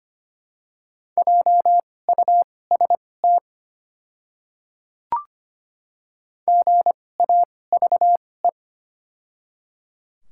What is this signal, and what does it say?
Morse code, heard as a keyed tone, about 25 wpm, spaced as for 18 wpm, 690 Hz. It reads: JUST GAVE
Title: 2x Character Spacing - Top 500 Words - 25wpm